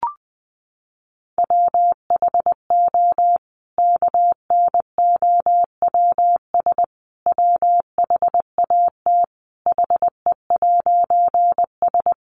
Morse code, heard as a keyed tone, about 20 wpm, 700 Hz. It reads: WHO KNOWS WHAT HE'S
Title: A Sound Of Thunder 20wpm